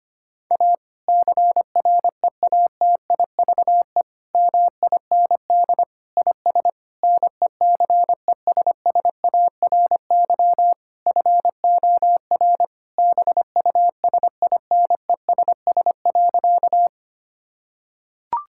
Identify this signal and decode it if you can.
Morse code, heard as a keyed tone, about 25 wpm, 700 Hz. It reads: A CREATIVE MIND IS NECESSARY FOR BUSINESS.
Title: Sentences from Top 1000 Words - Review Part 5 - 25wpm